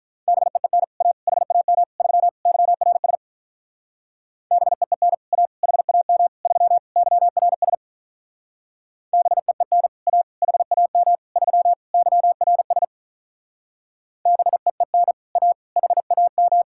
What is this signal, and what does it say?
Morse code, heard as a keyed tone, about 43 wpm, 700 Hz. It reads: BEEN A HAM 3 YRS BEEN A HAM 3 YRS BEEN A HAM 3 YRS BEEN A HAM
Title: Speed-Racing Common QSO Phrases - 35wpm